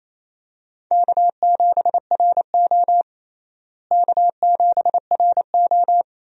Morse code, heard as a keyed tone, about 28 wpm, 700 Hz. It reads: K7RO K7RO
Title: USA Call Signs - 4 Characters - Repeated 2x - 28wpm